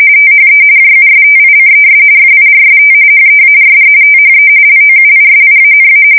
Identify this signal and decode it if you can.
RTTY, about 45.45 baud. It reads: THE JUSTICES FIRST HEARD THE LOUISIANA
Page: 170 Shift 2125/2295